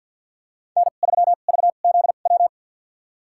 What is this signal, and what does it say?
Morse code, heard as a keyed tone, about 51 wpm, spaced as for 36 wpm, 700 Hz. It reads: N3VZW
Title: USA Call Signs - 5 Characters - 2x Character Spacing - 50wpm